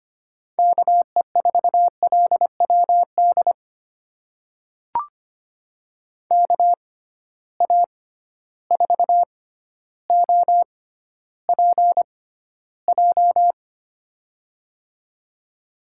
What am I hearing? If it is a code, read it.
Morse code, heard as a keyed tone, about 25 wpm, spaced as for 13 wpm, 700 Hz. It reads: KE4LWD KA4OPJ
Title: USA Call Signs - 6 Characters - 6x Character Spacing - 25wpm